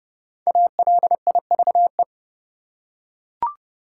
Morse code, heard as a keyed tone, about 31 wpm, 700 Hz. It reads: ALIVE
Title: Five Letter Words - 30wpm